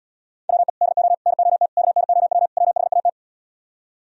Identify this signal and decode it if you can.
Morse code, heard as a keyed tone, about 77 wpm, 700 Hz. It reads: WE DO NOT KNOW WHAT